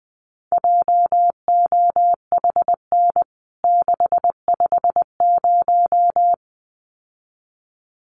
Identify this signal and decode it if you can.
Morse code, heard as a keyed tone, about 20 wpm, 700 Hz. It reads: JOHN 650
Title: CWT Exchange 20wpm